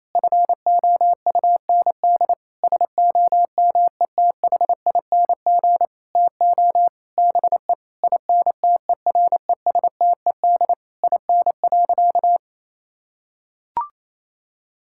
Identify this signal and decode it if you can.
Morse code, heard as a keyed tone, about 28 wpm, 700 Hz. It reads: FOUND SOMETHING TO BE INTERESTED IN.
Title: Sentences from Top 1000 Words - Review Part 6 - 28wpm